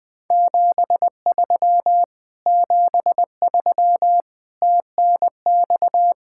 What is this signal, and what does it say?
Morse code, heard as a keyed tone, about 19 wpm, 700 Hz. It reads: 73 73 TNX
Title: Speed-Racing Common QSO Phrases - 20wpm